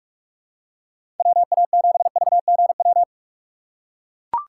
Morse code, heard as a keyed tone, about 43 wpm, 700 Hz. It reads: WA7VGW
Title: USA Call Signs - 6 Characters - 4x Character Spacing - 45wpm